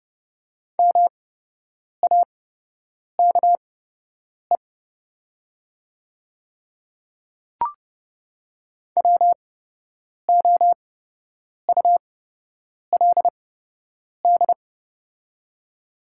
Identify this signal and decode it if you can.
Morse code, heard as a keyed tone, about 30 wpm, spaced as for 8 wpm, 700 Hz. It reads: MAKE WOULD
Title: 8x Character Spacing - Top 100 Words - 30wpm